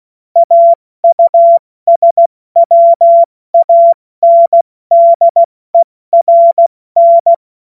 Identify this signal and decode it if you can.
Morse code, heard as a keyed tone, about 16 wpm, spaced as for 14 wpm, 680 Hz. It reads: AUSWANDERN
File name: Morsekurs / Witze / Witz 19. April 2022 70 F80